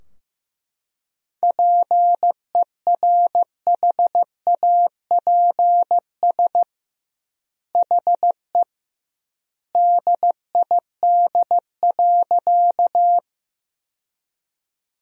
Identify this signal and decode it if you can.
Morse code, heard as a keyed tone, about 15 wpm, 700 Hz. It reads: PERHAPS HE DID.
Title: 2x Word-Spacing - Sentences from Top 500 Words - 15wpm